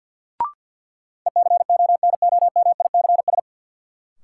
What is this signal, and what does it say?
Morse code, heard as a keyed tone, about 53 wpm, 700 Hz. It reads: ECONOMICS
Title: Sentences from Top 1000 Words - Review Part 3 - New Word Drill - 50wpm